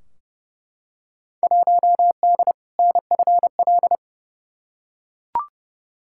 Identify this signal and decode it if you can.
Morse code, heard as a keyed tone, about 30 wpm, 700 Hz. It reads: 1D NFL
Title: Field Day Exchanges - 30wpm